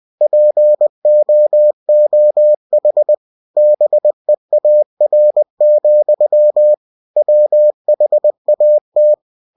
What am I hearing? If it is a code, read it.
Morse code, heard as a keyed tone, about 20 wpm, 600 Hz. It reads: POOH BEAR, WHAT
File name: cw-books / winnie-the-pooh / winnie-20wpm-page-0099